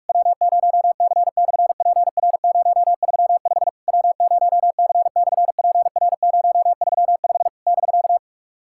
Morse code, heard as a keyed tone, about 44 wpm, 700 Hz. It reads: W0C/PR035 W0C/PR035 <BK>
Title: SOTA QSO Practice 45wpm